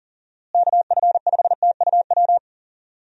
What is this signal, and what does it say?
Morse code, heard as a keyed tone, about 39 wpm, 700 Hz. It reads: KF5TUW
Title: USA Call Signs - 6 Characters - Repeated 3x - 40wpm